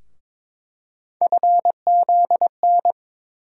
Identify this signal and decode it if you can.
Morse code, heard as a keyed tone, about 22 wpm, 700 Hz. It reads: FZN
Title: USA Call Sign Postfixes - 3 Letters 22wpm